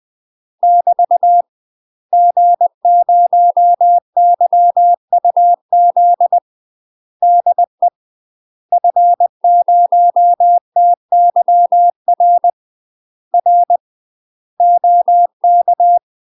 Morse code, heard as a keyed tone, about 20 wpm, 700 Hz. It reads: <BT> G0YUZ DE F0TYR R OK
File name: UK MASTER QSOs TRAINING / 20WPM / UK MASTER QSOs 0067